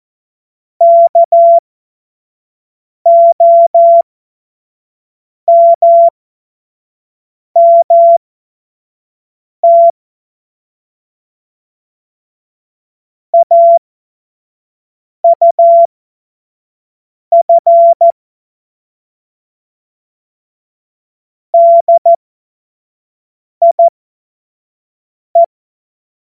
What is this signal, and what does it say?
Morse code, heard as a keyed tone, about 14 wpm, spaced as for 5 wpm, 680 Hz. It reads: KOMMT AUF DIE
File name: Morsekurs / Witze / Witz 30. Dezember 2025 25 F70